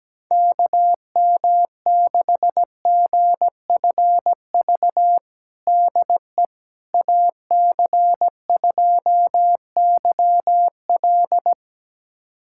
Morse code, heard as a keyed tone, about 17 wpm, 700 Hz. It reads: KM6GFV DE AC2YL
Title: USA Callsign DE Callsign 17wpm